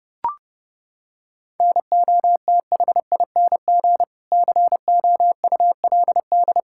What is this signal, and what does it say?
Morse code, heard as a keyed tone, about 30 wpm, 700 Hz. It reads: NOTHING COULD
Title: Sentences from Top 1600 Words 30wpm